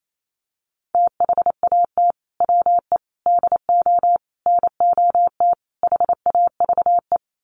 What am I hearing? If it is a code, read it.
Morse code, heard as a keyed tone, about 28 wpm, 700 Hz. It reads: THAT WE DO NOT HAVE
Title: Sets of 5 Words Encore 28wpm